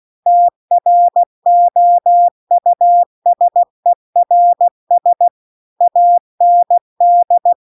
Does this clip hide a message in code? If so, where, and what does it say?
Yes: Morse code from 0.3 to 7.5 s, heard as a keyed tone, about 16 wpm, 700 Hz. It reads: TROUSERS AND